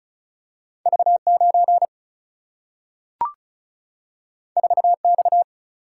Morse code, heard as a keyed tone, about 36 wpm, 700 Hz. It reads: V 9 4 X
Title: Single-Letter Single-Number 35wpm